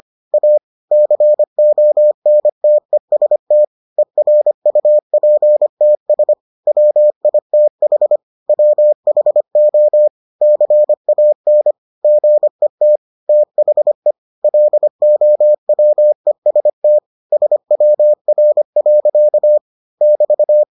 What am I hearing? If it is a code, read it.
Morse code, heard as a keyed tone, about 25 wpm, 600 Hz. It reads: A CONTEST ERUPTS WITH WHO CAN GET THE LOWEST SWR. <BT>